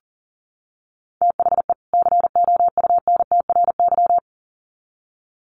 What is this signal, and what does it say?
Morse code, heard as a keyed tone, about 40 wpm, 700 Hz. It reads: THE COUNTRY
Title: Sets of 2 Words 40wpm